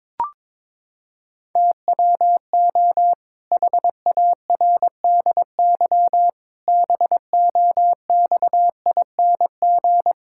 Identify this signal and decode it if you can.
Morse code, heard as a keyed tone, about 22 wpm, 700 Hz. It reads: TWO HARDY BOXING
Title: Pangram Challenge v2 - 22wpm